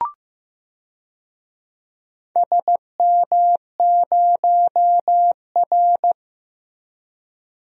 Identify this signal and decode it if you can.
Morse code, heard as a keyed tone, about 15 wpm, 700 Hz. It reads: SM0R
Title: DXCC-100 Call Signs - 4 Characters - Repeated 2x - 15wpm